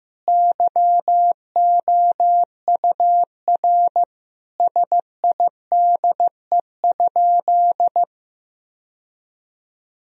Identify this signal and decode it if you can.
Morse code, heard as a keyed tone, about 15 wpm, 700 Hz. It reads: YOUR SIDE?
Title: Sentences from Top 1000 Words - Review Part 1 - 15wpm